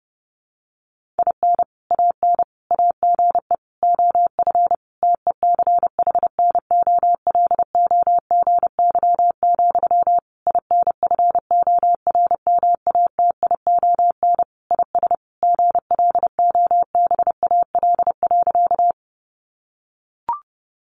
Morse code, heard as a keyed tone, about 28 wpm, 700 Hz. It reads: IN AN AGE OF TECHNOLOGY, INFORMATION IS GLOBAL.
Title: Sentences from Top 1200 Words 30wpm